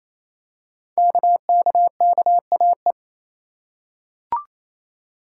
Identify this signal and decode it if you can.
Morse code, heard as a keyed tone, about 27 wpm, 700 Hz. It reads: KKKAE
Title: Code Groups - 5 Letters - 28wpm